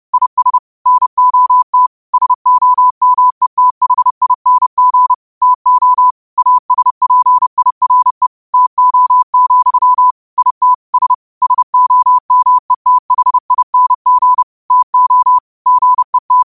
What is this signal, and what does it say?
Morse code, heard as a keyed tone, about 30 wpm, 1000 Hz. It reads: IS NOT SOMETHING TO ASPIRE TO, IT S SOMETHING TO GET
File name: qotdfiles / 30wpm